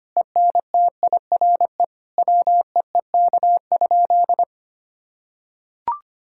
Morse code, heard as a keyed tone, about 25 wpm, 700 Hz. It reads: ENTIRE WEEK?
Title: Sentences from Top 1000 Words - Review Part 5 - 25wpm